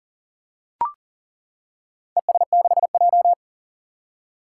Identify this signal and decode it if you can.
Morse code, heard as a keyed tone, about 38 wpm, 700 Hz. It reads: ES6J
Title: Four Challenging Letter-Number Combinations 40wpm